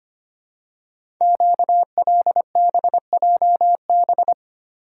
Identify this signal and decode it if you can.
Morse code, heard as a keyed tone, about 25 wpm, 700 Hz. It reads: QLBJB
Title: Code Groups - 5 Letters - 25wpm